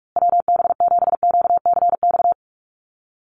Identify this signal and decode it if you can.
Morse code, heard as a keyed tone, about 44 wpm, 700 Hz. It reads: WB7QCX
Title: USA Call Signs Repeated 2x 45wpm